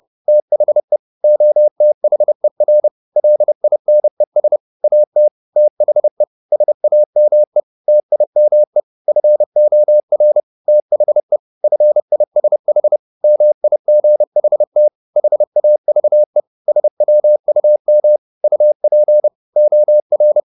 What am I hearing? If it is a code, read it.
Morse code, heard as a keyed tone, about 30 wpm, 600 Hz. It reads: THE OTHER LINES AT THE SAME TIME FOR THE FISH MIGHT HAVE SWUM UP OR
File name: the old man and the sea / the old man and the sea 1 30wpm